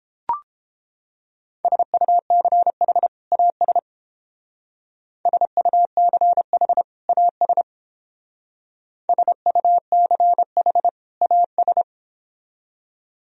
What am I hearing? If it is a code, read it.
Morse code, heard as a keyed tone, about 31 wpm, 700 Hz. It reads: SUCH AS SUCH AS SUCH AS